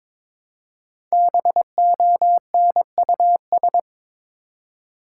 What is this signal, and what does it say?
Morse code, heard as a keyed tone, about 22 wpm, 700 Hz. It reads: BONUS
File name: 5 Letter Words - First 3 Letters Given - 22wpm